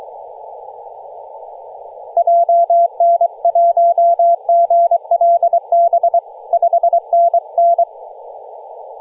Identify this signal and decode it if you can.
Morse code, heard as a keyed tone, about 23 wpm, 680 Hz. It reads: JN1GLB 5NN
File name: posts / 2016-nov-13-0348z-t30jy-28-034mhz-cw